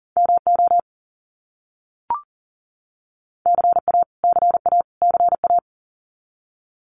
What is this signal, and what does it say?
Morse code, heard as a keyed tone, about 40 wpm, 700 Hz. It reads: MO CA CA CA
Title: US States Repeated 3x 40wpm